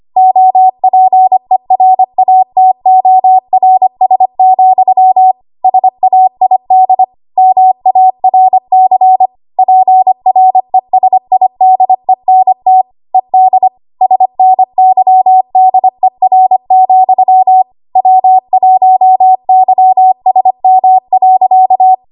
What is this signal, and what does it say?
Morse code, heard as a keyed tone, about 25 wpm, 750 Hz. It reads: OPERATORS, SAID MARC PRESIDENT ED SNYDER, W1YSM.
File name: Morse / 25WPM